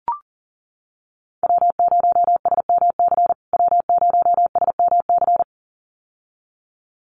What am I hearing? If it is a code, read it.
Morse code, heard as a keyed tone, about 40 wpm, 700 Hz. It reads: W0SMC W0SMC
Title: USA Call Signs Repeated 2x 40wpm